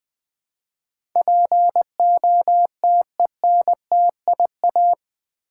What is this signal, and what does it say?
Morse code, heard as a keyed tone, about 20 wpm, 700 Hz. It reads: POTENTIA
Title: Build A Word - 4 Letters At A Time - 11-Letter Words - 20wpm